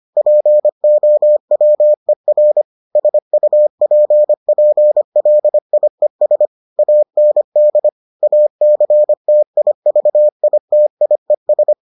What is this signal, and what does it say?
Morse code, heard as a keyed tone, about 25 wpm, 600 Hz. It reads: POWER SUPPLIES AND ACTIVITIES